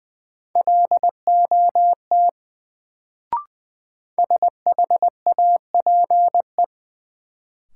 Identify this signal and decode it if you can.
Morse code, heard as a keyed tone, about 20 wpm, 700 Hz. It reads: LOT SHAPE
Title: Sentences from Top 500 Words - New Word Drill - 20wpm